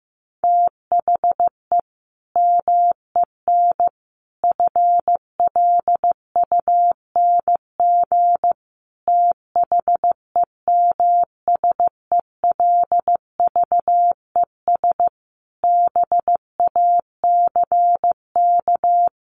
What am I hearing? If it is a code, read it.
Morse code, heard as a keyed tone, about 15 wpm, 700 Hz. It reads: THE MEN FLUNG THEMSELVES BACK
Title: A Sound Of Thunder 15wpm